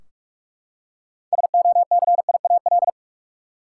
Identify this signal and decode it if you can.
Morse code, heard as a keyed tone, about 44 wpm, 700 Hz. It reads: SOCIAL
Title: Practice QSO - Exchange 3 - More Info - 45wpm